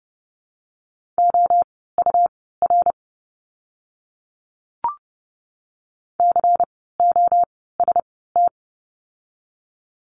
Morse code, heard as a keyed tone, about 30 wpm, spaced as for 17 wpm, 700 Hz. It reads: OUR COST
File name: Lesson 17f - Letter u - Words 30wpm